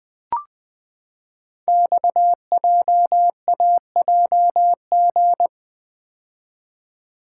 Morse code, heard as a keyed tone, about 20 wpm, 700 Hz. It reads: XJAJG
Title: Code Groups - 5 Letters - 20wpm